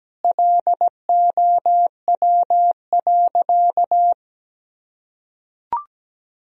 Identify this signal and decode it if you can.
Morse code, heard as a keyed tone, about 17 wpm, 700 Hz. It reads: LOW.
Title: Sentences from Top 1000 Words - Review Part 1 - 17wpm